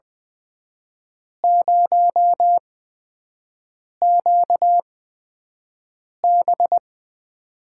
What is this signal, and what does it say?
Morse code, heard as a keyed tone, about 20 wpm, spaced as for 5 wpm, 700 Hz. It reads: 0QB
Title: USA Call Signs - 8x Character Spacing - 20wpm